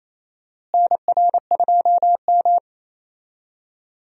Morse code, heard as a keyed tone, about 27 wpm, 700 Hz. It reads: NR2M
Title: USA Call Signs - 4 Characters - Repeated 3x - 28wpm